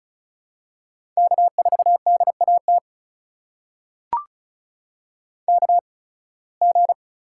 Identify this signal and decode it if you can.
Morse code, heard as a keyed tone, about 36 wpm, spaced as for 23 wpm, 700 Hz. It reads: K4DAT KG
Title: USA Call Signs - 8x Character Spacing - 35wpm